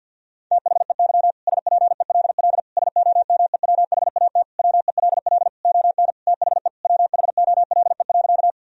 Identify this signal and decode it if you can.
Morse code, heard as a keyed tone, about 50 wpm, 700 Hz. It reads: THEY SPELL SOMEWHAT WELL ON THE WHOLE.
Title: Sentences from Top 1000 Words - Review Part 1 - 50wpm